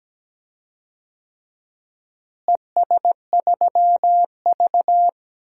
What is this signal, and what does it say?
Morse code, heard as a keyed tone, about 17 wpm, 700 Hz. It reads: ES3V
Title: DXCC-100 Call Signs - 4 Characters - Repeated 2x - 17wpm